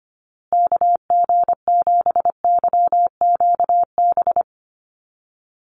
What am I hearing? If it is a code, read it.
Morse code, heard as a keyed tone, about 25 wpm, 700 Hz. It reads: KG7YQB
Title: USA Call Signs Repeated 2x 25wpm